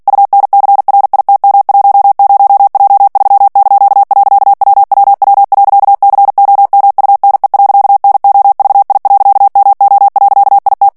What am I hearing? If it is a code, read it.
Morse code, heard as a keyed tone, about 48 wpm, 800 Hz. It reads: UNCRITM10J3).WWW.COMUNE.NOVI.MO.IT
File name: morse mp3 / ITALIEN texte clair 1 vitesse 45 mots minute